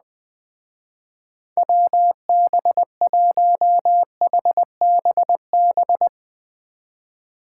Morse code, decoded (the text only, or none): WB1HBB